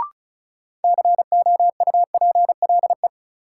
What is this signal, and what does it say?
Morse code, heard as a keyed tone, about 34 wpm, 700 Hz. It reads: COUPLE
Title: Sentences from Top 1100 Words - New Word Drill Alt - 35wpm